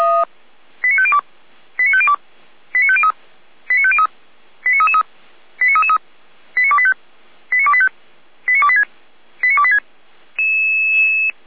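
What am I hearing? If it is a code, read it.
Five-tone selective call (ZVEI), heard as quick runs of five tones, tones of about 70 ms. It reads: 89502 89502 89503 89503 89333 89333 89285 89285 89286 89286
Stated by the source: ZVEI five-tone call received by SDR.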